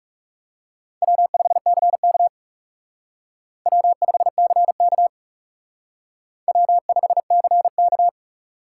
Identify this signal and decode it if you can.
Morse code, heard as a keyed tone, about 37 wpm, 700 Hz. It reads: W5CK W5CK W5CK
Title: Speed Racing - USA 1x2 Call Signs 30wpm